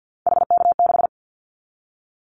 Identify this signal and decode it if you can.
Morse code, heard as a keyed tone, about 49 wpm, 700 Hz. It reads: HK6
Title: Callsign Prefixes for Top 100 DXCC - 50wpm